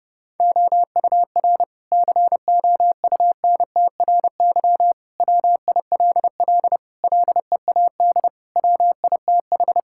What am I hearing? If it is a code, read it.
Morse code, heard as a keyed tone, about 32 wpm, 700 Hz. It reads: OUR COUNTRY WILL LEAD WITH
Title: Sentences from Top 1000 Words - Review Part 5 - 30wpm